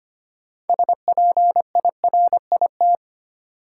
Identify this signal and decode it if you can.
Morse code, heard as a keyed tone, about 25 wpm, 700 Hz. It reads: SPIRIT
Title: Sentences from Top 1400 Words - New Word Drill Alt - 25wpm